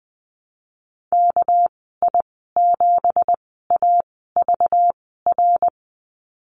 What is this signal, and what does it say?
Morse code, heard as a keyed tone, about 20 wpm, spaced as for 15 wpm, 700 Hz. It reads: KI7AVR
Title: 200 Lesson 32g - Letter k - Call Signs 20wpm